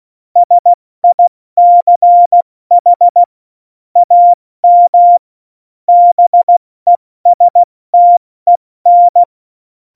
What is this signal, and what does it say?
Morse code, heard as a keyed tone, about 16 wpm, spaced as for 14 wpm, 700 Hz. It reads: SICH AM BESTEN